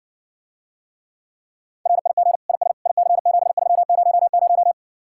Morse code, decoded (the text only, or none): RIG IS IC7300